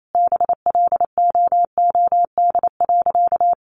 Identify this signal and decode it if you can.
Morse code, heard as a keyed tone, about 28 wpm, 700 Hz. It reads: BLOOD.
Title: A Sound Of Thunder 28wpm